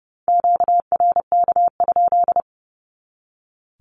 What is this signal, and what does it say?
Morse code, heard as a keyed tone, about 30 wpm, 700 Hz. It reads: QRK?
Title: Q-Codes 30wpm